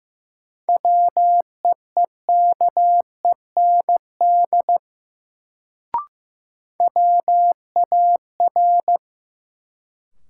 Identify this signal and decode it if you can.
Morse code, heard as a keyed tone, about 15 wpm, 700 Hz. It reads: WEEKEND WAR
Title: Sets of 5 Words - New Word Drill - 15wpm